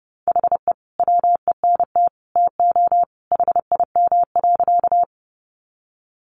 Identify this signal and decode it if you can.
Morse code, heard as a keyed tone, about 30 wpm, 700 Hz. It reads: HE WENT TO HIM.
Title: Sentences from Top 200 Words 30wpm